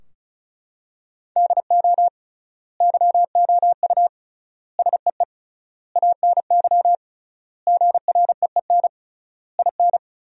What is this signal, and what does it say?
Morse code, heard as a keyed tone, about 34 wpm, 700 Hz. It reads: DO YOU SEE ANY GREEN IN
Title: 3x Word-Spacing - Sentences from Top 500 Words - 35wpm